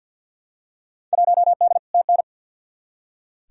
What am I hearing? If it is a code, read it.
Morse code, heard as a keyed tone, about 47 wpm, 700 Hz. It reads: 1D TN
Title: Field Day Exchanges - 50wpm